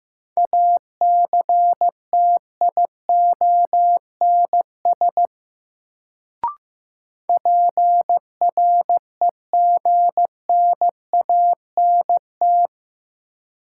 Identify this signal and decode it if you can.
Morse code, heard as a keyed tone, about 15 wpm, 700 Hz. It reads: ACTIONS PREGNANT
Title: Sentences from Top 1000 Words - Review Part 6 - New Word Drill - 15wpm